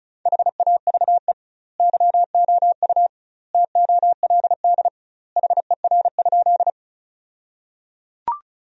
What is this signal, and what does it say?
Morse code, heard as a keyed tone, about 34 wpm, 700 Hz. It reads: HAVE YOU TOLD HER?
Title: 2x Word-Spacing - Sentences from Top 400 Words - 35wpm